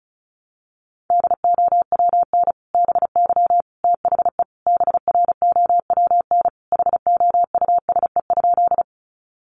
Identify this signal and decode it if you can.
Morse code, heard as a keyed tone, about 34 wpm, 700 Hz. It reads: DOWN BY THE BROWN HOUSE?
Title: Sentences from Top 1200 Words 35wpm